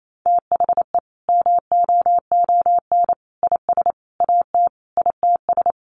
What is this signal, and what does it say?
Morse code, heard as a keyed tone, about 28 wpm, 700 Hz. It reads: THE MOON IS AT ITS
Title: Sentences from Top 1300 Words 28wpm